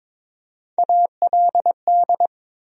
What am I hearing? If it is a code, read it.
Morse code, heard as a keyed tone, about 22 wpm, 700 Hz. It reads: ALD
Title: USA Call Sign Postfixes - 3 Letters 22wpm